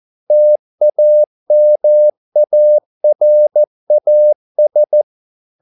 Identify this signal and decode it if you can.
Morse code, heard as a keyed tone, about 14 wpm, 600 Hz. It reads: TAMARAS